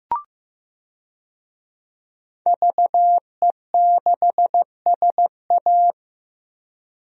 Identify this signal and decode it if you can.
Morse code, heard as a keyed tone, about 15 wpm, 700 Hz. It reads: VE6SA
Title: DXCC-100 Call Signs - 5 Characters - Repeated 3x - 15wpm